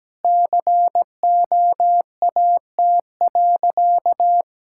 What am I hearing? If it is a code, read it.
Morse code, heard as a keyed tone, about 17 wpm, 700 Hz. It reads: COAT.
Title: Sentences from Top 1600 Words 17wpm